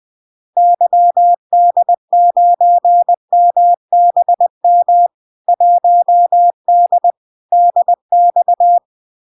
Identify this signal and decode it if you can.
Morse code, heard as a keyed tone, about 20 wpm, 700 Hz. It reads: YD9MBM 1D DX
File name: LARK-FD-2022-Log-in-CW-20wpm-700-Hz